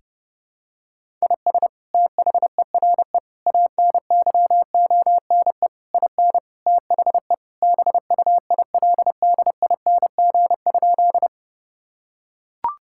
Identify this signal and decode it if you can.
Morse code, heard as a keyed tone, about 32 wpm, 700 Hz. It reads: IS THERE ANYONE IN THE BUILDING?
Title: Sentences from Top 1000 Words - Review Part 5 - 30wpm